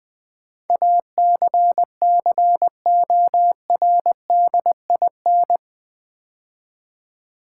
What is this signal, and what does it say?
Morse code, heard as a keyed tone, about 20 wpm, 700 Hz. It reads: ACCORDIN
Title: Build A Word - 4 Letters At A Time - 11-Letter Words - 20wpm